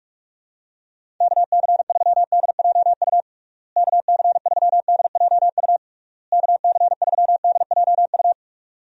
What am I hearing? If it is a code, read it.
Morse code, heard as a keyed tone, about 46 wpm, 700 Hz. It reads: KC3DJU KC3DJU KC3DJU
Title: USA Call Signs - 6 Characters - Repeated 3x - 45wpm